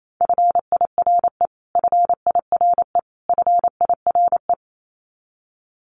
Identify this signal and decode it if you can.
Morse code, heard as a keyed tone, about 28 wpm, 700 Hz. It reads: FIRE FIRE FIRE
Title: Top 500 Words Repeated 28wpm